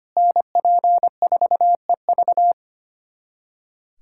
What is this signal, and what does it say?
Morse code, heard as a keyed tone, about 25 wpm, 700 Hz. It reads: NP4EV
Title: USA Callsign DE Callsign 25wpm